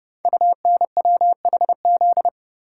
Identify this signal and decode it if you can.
Morse code, heard as a keyed tone, about 30 wpm, 700 Hz. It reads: UNWHZ
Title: Code Groups - 5 Letters - 30wpm